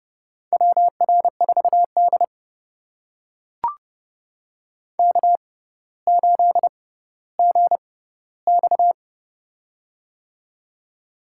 Morse code, heard as a keyed tone, about 29 wpm, spaced as for 15 wpm, 700 Hz. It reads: WR4D K8GX
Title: USA Call Signs - 4 Characters - 6x Character Spacing - 30wpm